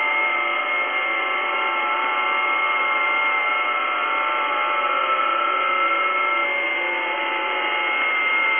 FT8.
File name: tests / websdr test12